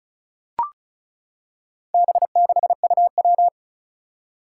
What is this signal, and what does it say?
Morse code, heard as a keyed tone, about 36 wpm, 700 Hz. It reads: B6UW
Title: Four Challenging Letter-Number Combinations 35wpm